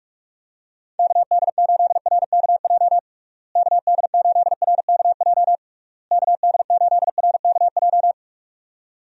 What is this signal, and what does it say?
Morse code, heard as a keyed tone, about 46 wpm, 700 Hz. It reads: KD8RKJ KD8RKJ KD8RKJ
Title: USA Call Signs - 6 Characters - Repeated 3x - 45wpm